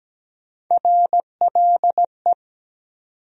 Morse code, heard as a keyed tone, about 17 wpm, 700 Hz. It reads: RLE
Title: USA Call Sign Postfixes - 3 Letters 17wpm